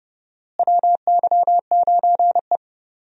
Morse code, heard as a keyed tone, about 30 wpm, 700 Hz. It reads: WY9E
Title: USA Call Signs - 4 Characters - Repeated 2x - 30wpm